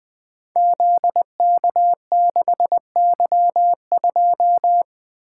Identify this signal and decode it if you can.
Morse code, heard as a keyed tone, about 20 wpm, 700 Hz. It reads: ZK6Y2
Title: Code Groups - 5 Letter:Numbers - 20wpm